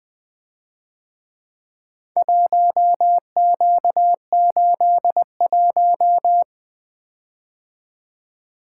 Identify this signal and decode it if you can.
Morse code, heard as a keyed tone, about 24 wpm, 700 Hz. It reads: 1Q81
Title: Speed-Racing - Four Letter-Number Combinations - 20wpm